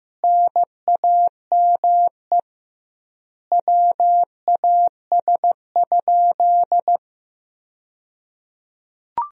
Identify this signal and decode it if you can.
Morse code, heard as a keyed tone, about 16 wpm, 700 Hz. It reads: NAME WAS?
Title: 2x Word-Spacing - Sentences from Top 200 Words - 15wpm